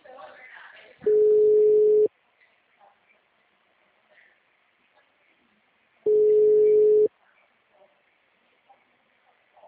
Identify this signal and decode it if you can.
Ringback tone (Europe, 425 Hz), with speech.